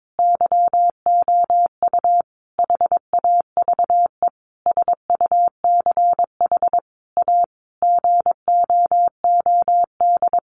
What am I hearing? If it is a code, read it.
Morse code, heard as a keyed tone, about 22 wpm, 700 Hz. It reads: YOU HAVE SUCH A GOOD
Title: Sentences from Top 1300 Words 22wpm